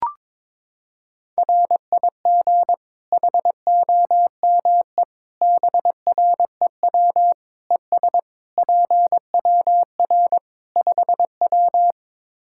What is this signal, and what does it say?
Morse code, heard as a keyed tone, about 21 wpm, 700 Hz. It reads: RIG HOME BREW ES PWR 5W
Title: Practice QSO - Exchange 2 - Info - 22wpm